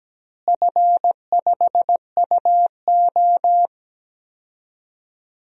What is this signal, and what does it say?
Morse code, heard as a keyed tone, about 17 wpm, 700 Hz. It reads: F5UO
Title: DXCC-100 Call Signs - 4 Characters - Repeated 2x - 17wpm